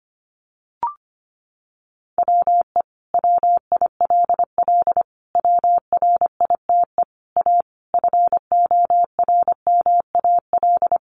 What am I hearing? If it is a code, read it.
Morse code, heard as a keyed tone, about 25 wpm, 700 Hz. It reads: WE WILL WRITE A FORMAL
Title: Sentences from Top 1300 Words 25wpm